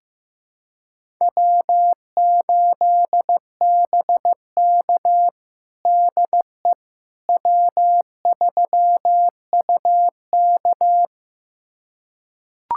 Morse code, heard as a keyed tone, about 16 wpm, 700 Hz. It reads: W8BK DE W3UK
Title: USA Callsign DE Callsign - 4 Characters - 15wpm